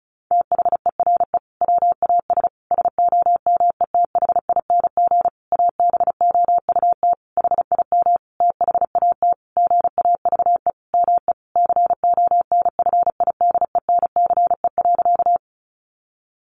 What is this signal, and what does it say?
Morse code, heard as a keyed tone, about 34 wpm, 700 Hz. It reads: THERE WAS SOMETHING ABOUT HIM THAT GAVE ME CONFIDENCE.
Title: Sentences from Top 1500 Words 35wpm